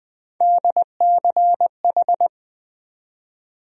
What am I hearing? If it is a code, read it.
Morse code, heard as a keyed tone, about 20 wpm, 700 Hz. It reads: DCH
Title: USA Call Sign Postfixes - 3 Letters 20wpm